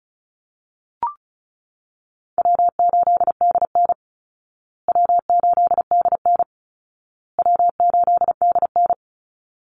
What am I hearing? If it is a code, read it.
Morse code, heard as a keyed tone, about 34 wpm, 700 Hz. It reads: W8DN W8DN W8DN
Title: Easy Call Signs Repeated 3x - 35wpm